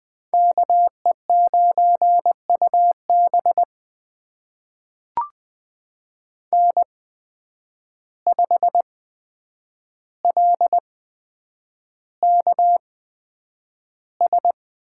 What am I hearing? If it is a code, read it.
Morse code, heard as a keyed tone, about 20 wpm, spaced as for 9 wpm, 700 Hz. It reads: KE9UB N5LKS
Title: USA Call Signs - 5 Characters - 8x Character Spacing - 20wpm